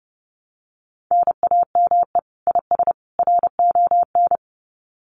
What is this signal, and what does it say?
Morse code, heard as a keyed tone, about 30 wpm, 700 Hz. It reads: NAME IS RON
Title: Common QSO Phrases 30wpm